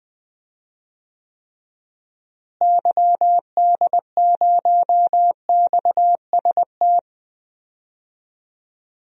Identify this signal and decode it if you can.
Morse code, heard as a keyed tone, about 20 wpm, 700 Hz. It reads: YD0XST
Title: DXCC-100 Call Signs - 6 Characters - Repeated 3x - 20wpm